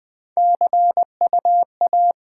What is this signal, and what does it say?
Morse code, heard as a keyed tone, about 20 wpm, 700 Hz. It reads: CU A
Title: Top 20 QSO Elements - 8x Character Spacing - 20wpm